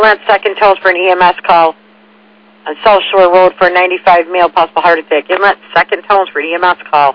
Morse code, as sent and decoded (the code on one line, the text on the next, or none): none